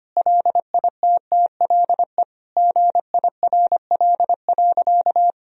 Morse code, heard as a keyed tone, about 25 wpm, 700 Hz. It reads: LITTLE GIRL.
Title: Sentences from Top 1000 Words - Review Part 1 - 25wpm